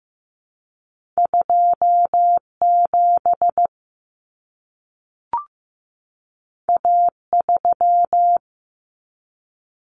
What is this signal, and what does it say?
Morse code, heard as a keyed tone, about 15 wpm, 700 Hz. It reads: 27 A3
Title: Common Cut Numbers - 2-Digit 15wpm